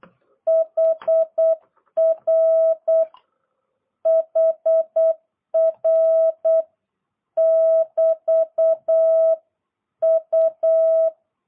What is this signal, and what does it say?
Morse code, heard as a keyed tone, about 8 wpm, 660 Hz. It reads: HR HR <BT> U